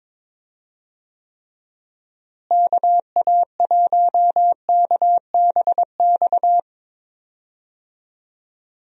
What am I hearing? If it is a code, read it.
Morse code, heard as a keyed tone, about 22 wpm, 700 Hz. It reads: KA1KBX
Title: USA Call Signs - 6 Characters - Repeated 2x - 22wpm